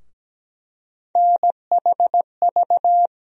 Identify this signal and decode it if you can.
Morse code, heard as a keyed tone, about 17 wpm, 700 Hz. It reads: NHV
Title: USA Call Sign Postfixes - 3 Letters 17wpm